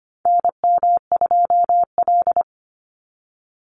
Morse code, heard as a keyed tone, about 25 wpm, 700 Hz. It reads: NM2L
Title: Call Signs Repeated 3x - 25wpm